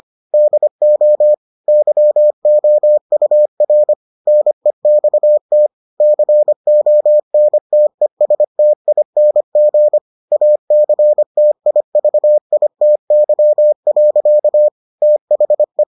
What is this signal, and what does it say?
Morse code, heard as a keyed tone, about 25 wpm, 600 Hz. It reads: DO YOUR NEXT CONTESTING ACTIVITY. THE